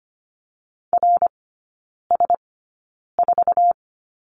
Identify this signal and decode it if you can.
Morse code, heard as a keyed tone, about 25 wpm, spaced as for 9 wpm, 690 Hz. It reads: RS4
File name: Lesson 15e - Number 5 - Three Characters 25wpm